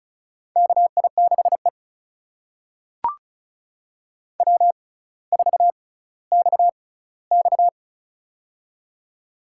Morse code, heard as a keyed tone, about 36 wpm, spaced as for 18 wpm, 700 Hz. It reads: KI6E W4XX
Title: USA Call Signs - 4 Characters - 6x Character Spacing - 35wpm